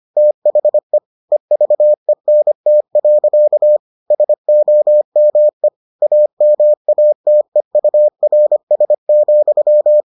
Morse code, heard as a keyed tone, about 25 wpm, 600 Hz. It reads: THE EVENT. SOME AMATEURS,